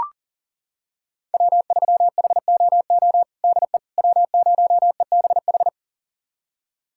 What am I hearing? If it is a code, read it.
Morse code, heard as a keyed tone, about 40 wpm, 700 Hz. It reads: W3HOO DE W0EBH
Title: USA Callsign DE Callsign - 5 Characters - 40wpm